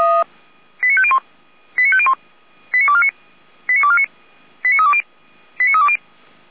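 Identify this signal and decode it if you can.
Five-tone selective call (ZVEI), heard as quick runs of five tones, tones of about 70 ms. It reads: 89501 89501 89249 89249 89320 89320
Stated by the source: ZVEI five-tone call 89337, received by SDR, from the feuerwehr-orlamuende.org fire brigade audio server.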